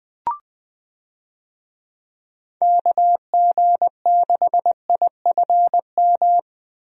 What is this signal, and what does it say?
Morse code, heard as a keyed tone, about 20 wpm, 700 Hz. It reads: KG6IFM
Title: USA Call Signs - 6 Characters - Repeated 2x - 20wpm